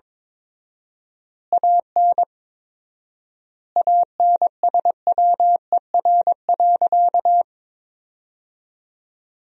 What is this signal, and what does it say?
Morse code, heard as a keyed tone, about 22 wpm, 700 Hz. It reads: AN ANSWER.
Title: 4x Word-Spacing - Sentences from Top 300 Words - 22wpm